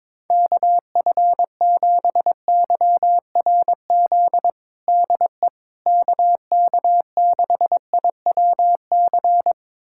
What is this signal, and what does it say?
Morse code, heard as a keyed tone, about 22 wpm, 700 Hz. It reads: KF7YRZ DE KK6IWC
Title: USA Callsign DE Callsign - 6 Characters - 22wpm